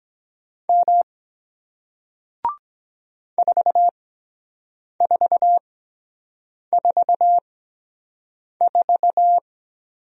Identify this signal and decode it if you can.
Morse code, heard as a keyed tone, about 23 wpm, spaced as for 7 wpm, 700 Hz. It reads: M 4444